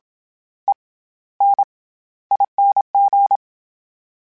Morse code, heard as a keyed tone, about 27 wpm, spaced as for 15 wpm, 800 Hz. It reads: ENING
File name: Special INGs-mp3s / Special INGs-mp3s / SlowToFaster ing @25